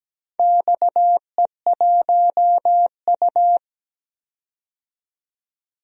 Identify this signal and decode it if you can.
Morse code, heard as a keyed tone, about 17 wpm, 700 Hz. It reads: XE1U
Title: DXCC-100 Call Signs - 4 Characters - 17wpm